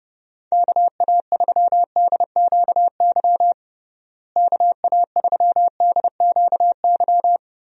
Morse code, heard as a keyed tone, about 30 wpm, 700 Hz. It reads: KA3DQY KA3DQY
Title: USA Call Signs - 6 Characters - Repeated 3x - 30wpm